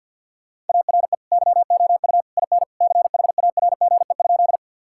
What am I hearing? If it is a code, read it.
Morse code, heard as a keyed tone, about 50 wpm, 700 Hz. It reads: ARE YOU IN CHARGE?
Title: Sentences from Top 1000 Words - Review Part 6 - 50wpm